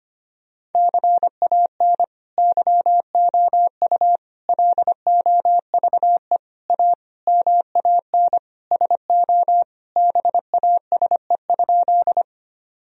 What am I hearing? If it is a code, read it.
Morse code, heard as a keyed tone, about 25 wpm, 700 Hz. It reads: CAN YOU LOVE A MAN SO BASE?
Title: Sentences from Top 1000 Words - Review Part 1 - 25wpm